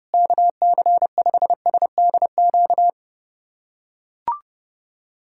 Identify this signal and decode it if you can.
Morse code, heard as a keyed tone, about 30 wpm, 700 Hz. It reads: KC5SDQ
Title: USA Callsign DE Callsign - 6 Characters - 30wpm